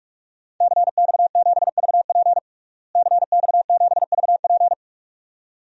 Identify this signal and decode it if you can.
Morse code, heard as a keyed tone, about 42 wpm, 700 Hz. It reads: CX7VP CX7VP
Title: DXCC-100 Call Signs - 5 Characters - Repeated 2x - 45wpm